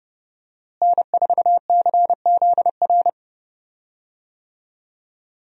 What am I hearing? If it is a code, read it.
Morse code, heard as a keyed tone, about 31 wpm, 700 Hz. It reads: N4CZR
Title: USA Call Signs - 5 Characters - Repeated 2x - 30wpm